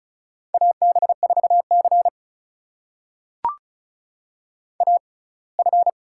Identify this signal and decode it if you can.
Morse code, heard as a keyed tone, about 36 wpm, spaced as for 24 wpm, 700 Hz. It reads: AB4C AF
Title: USA Call Signs - 4 Characters - 6x Character Spacing - 35wpm